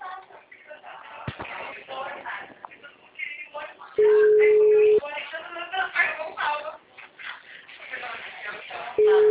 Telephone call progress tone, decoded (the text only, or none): ringback tone